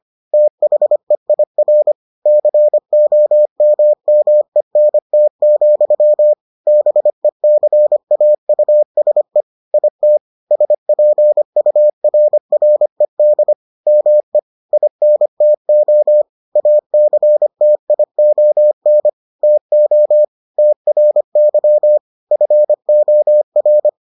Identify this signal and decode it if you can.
Morse code, heard as a keyed tone, about 25 wpm, 600 Hz. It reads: THEIR COMMENT, BECAUSE IT SPURRED ME INTO ACTION TO TRY FOR